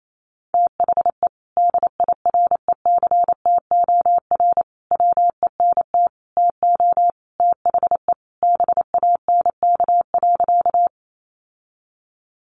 Morse code, heard as a keyed tone, about 28 wpm, 700 Hz. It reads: THE DIRECTOR WENT TO THE BANK.
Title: Sentences from Top 1300 Words 28wpm